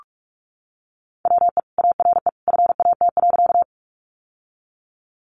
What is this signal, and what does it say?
Morse code, heard as a keyed tone, about 44 wpm, 700 Hz. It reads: WE ARE FAT.
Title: Sentences from Top 600 Words 45wpm